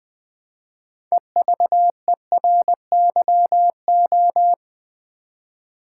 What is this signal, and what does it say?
Morse code, heard as a keyed tone, about 20 wpm, 700 Hz. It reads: EVERYO
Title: Build A Word - 2 Letters At A Time - 8-Letter Words - 20wpm